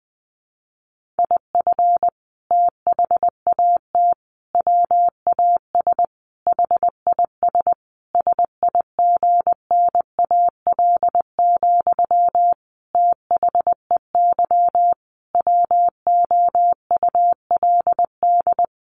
Morse code, heard as a keyed tone, about 20 wpm, 700 Hz. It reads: IF THAT WAS HIS SIGNAL, THEY WOULD
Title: Sentences from Top 1400 Words 20wpm